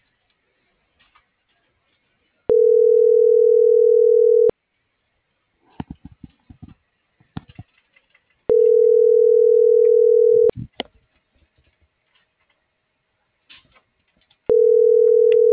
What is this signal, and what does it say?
Ringback tone (North America), 440 + 480 Hz.